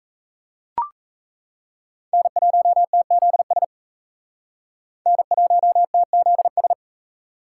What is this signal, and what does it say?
Morse code, heard as a keyed tone, about 35 wpm, 700 Hz. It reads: N1TZS N1TZS